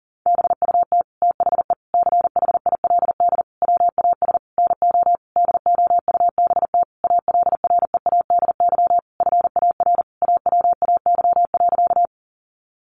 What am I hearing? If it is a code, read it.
Morse code, heard as a keyed tone, about 40 wpm, 700 Hz. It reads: BUT THE CHILD WAS NO DOUBT ALREADY FAR AWAY.
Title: Sentences from Top 1500 Words 40wpm